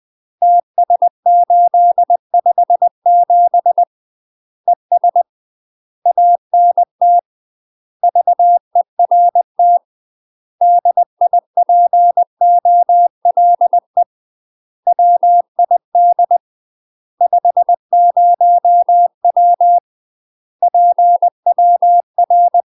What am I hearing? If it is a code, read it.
Morse code, heard as a keyed tone, about 20 wpm, 700 Hz. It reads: TS857 ES ANT VERT DIPOLE WID 50W PWR
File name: UK MASTER QSOs TRAINING / 20WPM / UK MASTER QSOs 0017